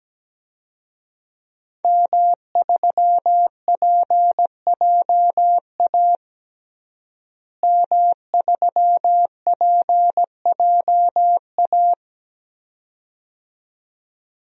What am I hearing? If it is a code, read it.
Morse code, heard as a keyed tone, about 17 wpm, 700 Hz. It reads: M3PJA M3PJA
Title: DXCC-100 Call Signs - 5 Characters - Repeated 2x - 17wpm